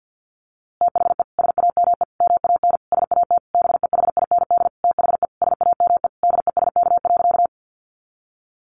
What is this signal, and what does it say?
Morse code, heard as a keyed tone, about 50 wpm, 700 Hz. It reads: THE SAME MAN SAT BEHIND THE SAME DESK.
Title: A Sound Of Thunder 50wpm